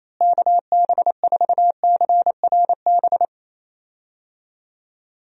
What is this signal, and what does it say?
Morse code, heard as a keyed tone, about 27 wpm, 700 Hz. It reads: KB4CRB
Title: USA Call Signs - 6 Characters - Repeated 2x - 28wpm